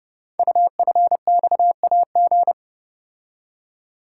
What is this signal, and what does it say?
Morse code, heard as a keyed tone, about 30 wpm, 690 Hz. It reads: UFXAG
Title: Code Groups - 5 Letters - 30wpm